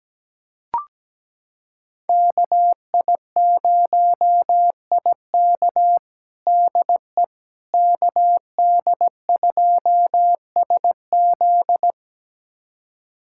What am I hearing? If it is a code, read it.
Morse code, heard as a keyed tone, about 17 wpm, 700 Hz. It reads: KI0IK DE KD2SZ
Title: USA Callsign DE Callsign - 5 Characters - 17wpm